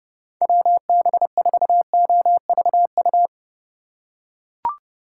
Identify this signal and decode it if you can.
Morse code, heard as a keyed tone, about 31 wpm, 700 Hz. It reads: WB4OVU